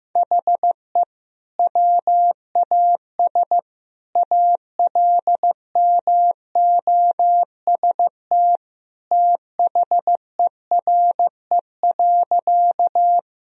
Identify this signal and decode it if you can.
Morse code, heard as a keyed tone, about 15 wpm, 700 Hz. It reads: HE WAS ALMOST THERE.
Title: Sentences from Top 1000 Words - Review Part 4 - 15wpm